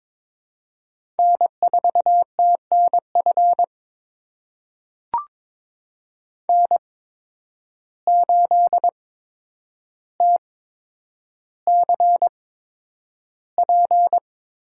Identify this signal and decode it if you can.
Morse code, heard as a keyed tone, about 21 wpm, spaced as for 9 wpm, 700 Hz. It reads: N4TNF N8TCP
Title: USA Call Signs - 5 Characters - 8x Character Spacing - 22wpm